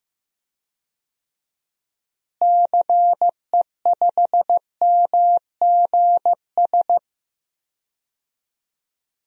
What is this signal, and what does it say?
Morse code, heard as a keyed tone, about 15 wpm, 700 Hz. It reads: CE5MGS
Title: DXCC-100 Call Signs - 6 Characters - Repeated 3x - 15wpm